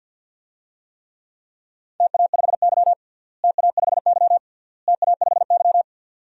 Morse code, heard as a keyed tone, about 51 wpm, 700 Hz. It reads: TA5Y TA5Y TA5Y
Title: DXCC-100 Call Signs - 4 Characters - Repeated 3x - 50wpm